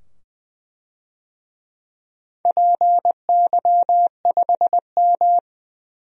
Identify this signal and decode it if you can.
Morse code, heard as a keyed tone, about 20 wpm, 700 Hz. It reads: PY5M
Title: DXCC-100 Call Signs - 4 Characters - 20wpm